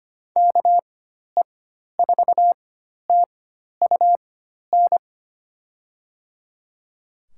Morse code, heard as a keyed tone, about 25 wpm, spaced as for 12 wpm, 700 Hz. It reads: KE4TUN
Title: USA Call Signs - 6 Characters - 4x Character Spacing - 25wpm